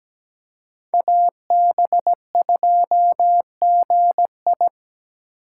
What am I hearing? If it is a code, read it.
Morse code, heard as a keyed tone, about 17 wpm, 700 Hz. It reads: AB2GI
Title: K1USN SST QSO Practice - 17wpm